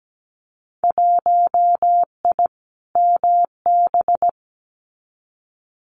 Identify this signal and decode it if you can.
Morse code, heard as a keyed tone, about 17 wpm, 700 Hz. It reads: 1I MB
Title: Winter Field Day Exchanges Encore 17wpm